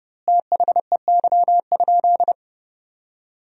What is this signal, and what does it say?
Morse code, heard as a keyed tone, about 30 wpm, 700 Hz. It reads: THEY?
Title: 2x Word-Spacing - Sentences from Top 300 Words - 30wpm